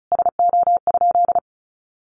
Morse code, heard as a keyed tone, about 34 wpm, 700 Hz. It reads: SO?
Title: Sentences from Top 300 Words 35wpm